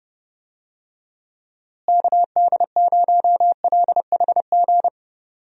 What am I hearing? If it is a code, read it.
Morse code, heard as a keyed tone, about 30 wpm, 700 Hz. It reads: KD0LHG
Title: USA Call Signs - 6 Characters - Repeated 2x - 30wpm